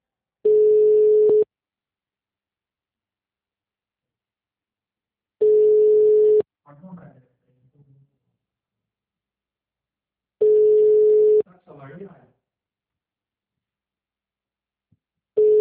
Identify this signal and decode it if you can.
Ringback tone (Europe, 425 Hz), with speech.